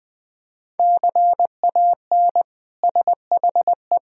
Morse code, heard as a keyed tone, about 21 wpm, 700 Hz. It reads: CAN SHE
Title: Sentences from Top 1000 Words - Review Part 4 - 20wpm